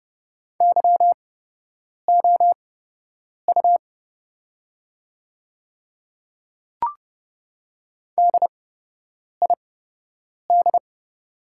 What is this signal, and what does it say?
Morse code, heard as a keyed tone, about 30 wpm, spaced as for 8 wpm, 700 Hz. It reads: YOU DID
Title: 8x Character Spacing - Top 100 Words - 30wpm